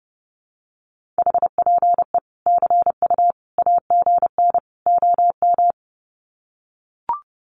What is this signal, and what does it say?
Morse code, heard as a keyed tone, about 30 wpm, 700 Hz. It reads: HPE CU AGN OM
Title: Common QSO Phrases 30wpm